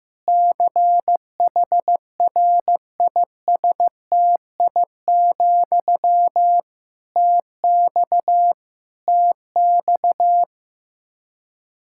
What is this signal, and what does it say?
Morse code, heard as a keyed tone, about 16 wpm, 700 Hz. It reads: CHRISTI, TX TX
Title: K1USN SST QSO Practice 15wpm